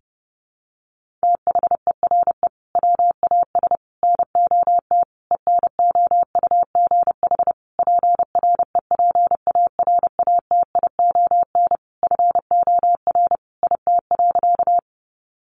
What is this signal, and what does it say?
Morse code, heard as a keyed tone, about 30 wpm, 700 Hz. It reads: THERE WAS NOT ENOUGH PREPARATION FOR IT.
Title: Sentences from Top 1400 Words 30wpm